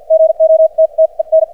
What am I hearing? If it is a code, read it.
Morse code, heard as a keyed tone, about 51 wpm, spaced as for 40 wpm, 590 Hz. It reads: GOTTEN
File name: archived / CRSnet